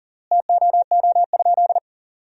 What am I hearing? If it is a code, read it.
Morse code, heard as a keyed tone, about 37 wpm, 700 Hz. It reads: TOO?
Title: 3x Word-Spacing - Sentences from Top 500 Words - 40wpm